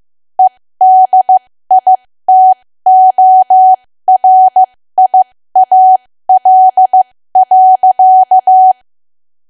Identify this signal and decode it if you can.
Morse code, heard as a keyed tone, about 15 wpm, spaced as for 13 wpm, 750 Hz. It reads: EDITORIAL.
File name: cwa / W1AW-Level2 / 13WPM